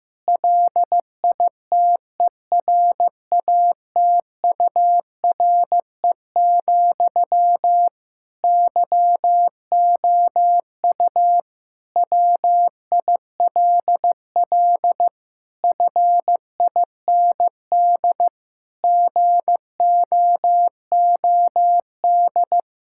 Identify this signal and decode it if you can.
Morse code, heard as a keyed tone, about 15 wpm, 700 Hz. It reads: LITERATURE, YOU WILL FIND GOOD
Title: Sentences from Top 1000 Words - Review Part 6 - 15wpm